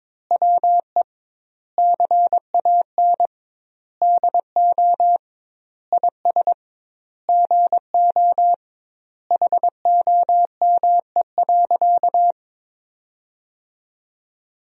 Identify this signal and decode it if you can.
Morse code, heard as a keyed tone, about 22 wpm, 700 Hz. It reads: WE CAN DO IS GO HOME.
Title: 2x Word-Spacing - Sentences from Top 200 Words - 22wpm